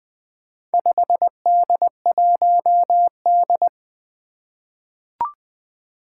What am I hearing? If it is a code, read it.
Morse code, heard as a keyed tone, about 20 wpm, 700 Hz. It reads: 5D1D
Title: Four Challenging Letter-Number Combinations 20wpm